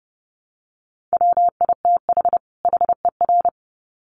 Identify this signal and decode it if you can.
Morse code, heard as a keyed tone, about 30 wpm, 700 Hz. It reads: WITH HER
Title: Sets of 2 Words 30wpm